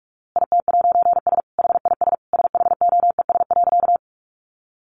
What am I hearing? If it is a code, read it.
Morse code, heard as a keyed tone, about 44 wpm, 700 Hz. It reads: IT'S HIS SHOES.
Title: A Sound Of Thunder 45wpm